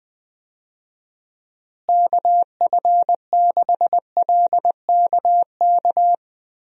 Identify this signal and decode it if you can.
Morse code, heard as a keyed tone, about 20 wpm, 700 Hz. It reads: KF6LKK
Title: USA Call Signs - 6 Characters - Repeated 2x - 20wpm